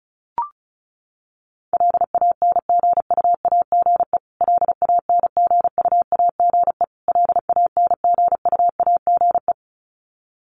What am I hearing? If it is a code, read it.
Morse code, heard as a keyed tone, about 34 wpm, 700 Hz. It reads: LANGUAGE LANGUAGE LANGUAGE
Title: Top 500 Words Repeated 35wpm